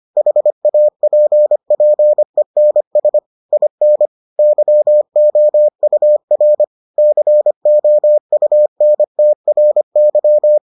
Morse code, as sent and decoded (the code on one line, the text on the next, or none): .... .- .--. .--. -. ... / .. -. / -.-- --- ..- .-. / -.-. --- ..- -. - .-. -.--
HAPPENS IN YOUR COUNTRY